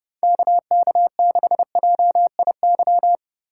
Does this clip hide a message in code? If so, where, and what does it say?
Yes: Morse code from 0.2 to 3.1 s, heard as a keyed tone, about 29 wpm, 700 Hz. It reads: KK6JIY